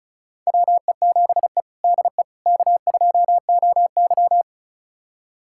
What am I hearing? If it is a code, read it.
Morse code, heard as a keyed tone, about 34 wpm, 700 Hz. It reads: WE7E DE K2OY
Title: USA Callsign DE Callsign - 4 Characters - 35wpm